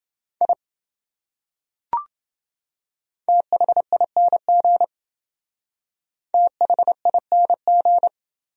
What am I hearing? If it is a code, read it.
Morse code, heard as a keyed tone, about 28 wpm, 700 Hz. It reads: I THING THING